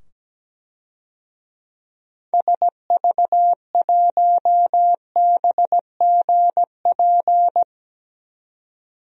Morse code, heard as a keyed tone, about 17 wpm, 700 Hz. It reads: SV1BGP
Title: DXCC-100 Call Signs - 6 Characters - 17wpm